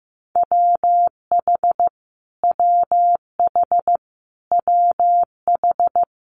Morse code, heard as a keyed tone, about 15 wpm, 700 Hz. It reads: WH WH WH
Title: Multi-letter Phonemes Repeated 3x 15wpm